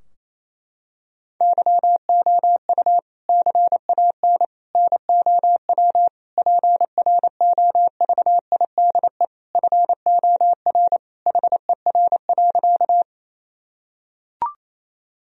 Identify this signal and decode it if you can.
Morse code, heard as a keyed tone, about 27 wpm, 700 Hz. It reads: YOU CAN NOW PROVIDE FOR HER.
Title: Sentences from Top 1000 Words - Review Part 3 - 28wpm